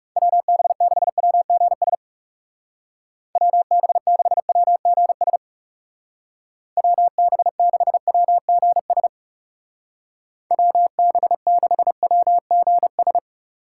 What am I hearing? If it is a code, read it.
Morse code, heard as a keyed tone, about 35 wpm, 700 Hz. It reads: WB6WGS WB6WGS WB6WGS WB6WGS
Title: Speed Racing - USA 2x3 Call Signs 30wpm